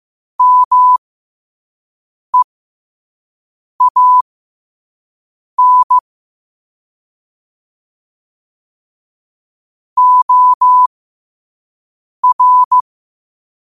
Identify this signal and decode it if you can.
Morse code, heard as a keyed tone, about 15 wpm, spaced as for 5 wpm, 1000 Hz. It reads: MEAN OR
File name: qotdfiles / 5wpm